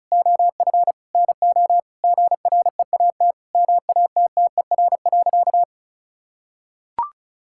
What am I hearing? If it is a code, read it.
Morse code, heard as a keyed tone, about 36 wpm, 700 Hz. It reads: OF NO GREAT MATTER.
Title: Sentences from Top 1600 Words 35wpm